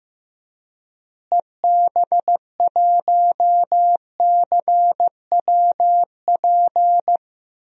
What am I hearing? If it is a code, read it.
Morse code, heard as a keyed tone, about 16 wpm, 700 Hz. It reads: EB1CWP
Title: DXCC-100 Call Signs - 6 Characters - Repeated 3x - 15wpm